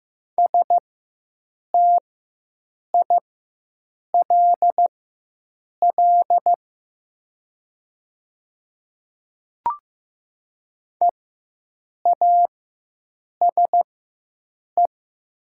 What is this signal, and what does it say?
Morse code, heard as a keyed tone, about 15 wpm, spaced as for 7 wpm, 700 Hz. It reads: STILL EASE
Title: 4x Character Spacing - Top 300 Words - 15wpm